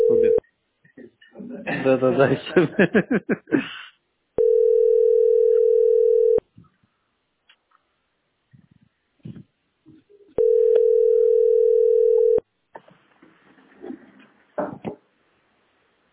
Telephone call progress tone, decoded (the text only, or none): ringback tone